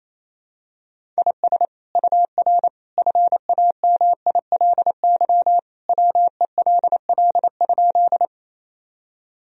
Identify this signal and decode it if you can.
Morse code, heard as a keyed tone, about 28 wpm, 700 Hz. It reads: IS UR FAMILY WELL?
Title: Sentences from Top 400 Words and Abbreviations 28wpm